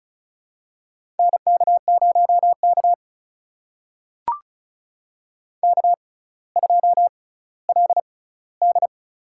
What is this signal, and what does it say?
Morse code, heard as a keyed tone, about 36 wpm, spaced as for 18 wpm, 700 Hz. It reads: NK0K K2LD
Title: USA Call Signs - 6x Character Spacing - 35wpm